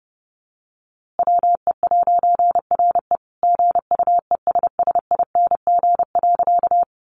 Morse code, heard as a keyed tone, about 30 wpm, 700 Hz. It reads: WE'RE GUESSING.
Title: A Sound Of Thunder 30wpm